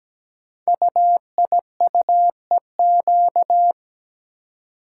Morse code, heard as a keyed tone, about 17 wpm, 700 Hz. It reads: UIUEQ
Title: Code Groups - 5 Letters - 17wpm